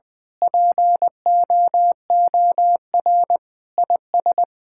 Morse code, heard as a keyed tone, about 20 wpm, 700 Hz. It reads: POOR IS
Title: Sentences with 8-Word Phrases - 20wpm